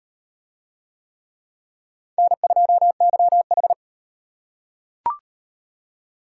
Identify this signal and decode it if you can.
Morse code, heard as a keyed tone, about 35 wpm, 700 Hz. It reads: N2YH